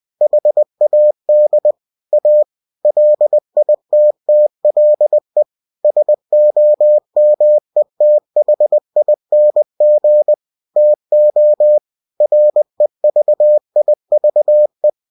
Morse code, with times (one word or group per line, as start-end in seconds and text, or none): 0.2-1.7 HAD
2.1-2.4 A
2.8-5.4 LITTLE
5.8-10.3 SOMETHING
10.8-11.8 TO
12.2-14.9 REVIVE